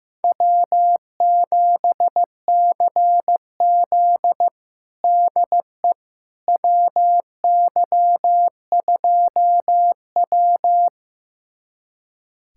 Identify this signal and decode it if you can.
Morse code, heard as a keyed tone, about 15 wpm, 700 Hz. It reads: W7CZ DE WY2W
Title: USA Callsign DE Callsign - 4 Characters - 15wpm